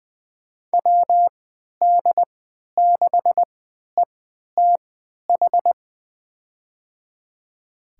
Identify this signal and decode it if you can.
Morse code, heard as a keyed tone, about 20 wpm, spaced as for 11 wpm, 700 Hz. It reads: WD6ETH
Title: USA Call Signs - 6 Characters - 3x Character Spacing - 20wpm